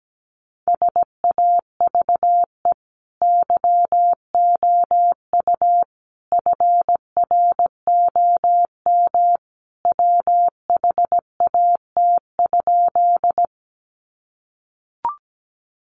Morse code, heard as a keyed tone, about 17 wpm, 700 Hz. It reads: SAVE YOU FROM WHAT?
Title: Sentences from Top 1100 Words 17wpm